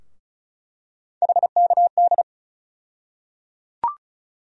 Morse code, heard as a keyed tone, about 36 wpm, 700 Hz. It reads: HKD
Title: USA Call Sign Postfixes - 3 Letters 35wpm